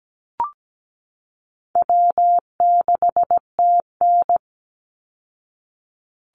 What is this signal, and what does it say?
Morse code, heard as a keyed tone, about 17 wpm, 700 Hz. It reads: W6TN
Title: Easy Call Signs Repeated 3x - 17wpm